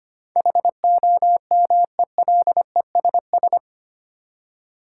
Morse code, heard as a keyed tone, about 25 wpm, 700 Hz. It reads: HOMELESS
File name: 8 Letter Words - First 3 Letters Given - 25wpm